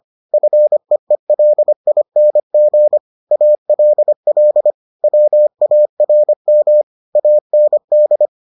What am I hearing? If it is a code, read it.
Morse code, heard as a keyed tone, about 25 wpm, 600 Hz. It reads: FEELING ALL WARM AND